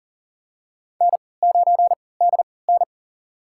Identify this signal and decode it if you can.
Morse code, heard as a keyed tone, about 40 wpm, spaced as for 23 wpm, 700 Hz. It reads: N9DN
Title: USA Call Signs - 4 Characters - 3x Character Spacing - 40wpm